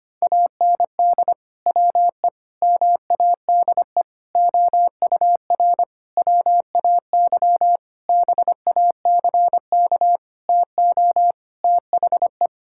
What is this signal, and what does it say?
Morse code, heard as a keyed tone, about 25 wpm, 700 Hz. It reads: AND WE MADE OUR WAY BACK TO THE
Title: Sentences from Top 1600 Words 25wpm